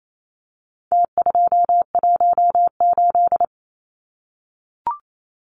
Morse code, heard as a keyed tone, about 28 wpm, 700 Hz. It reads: T218
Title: Numbers - Contest Style - 4-Digit 28wpm